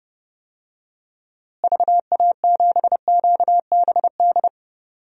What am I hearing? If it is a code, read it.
Morse code, heard as a keyed tone, about 30 wpm, 700 Hz. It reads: VA7QBD
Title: DXCC-100 Call Signs - 6 Characters - Repeated 3x - 30wpm